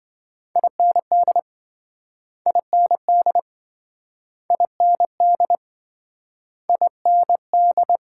Morse code, heard as a keyed tone, about 27 wpm, 700 Hz. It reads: IND IND IND IND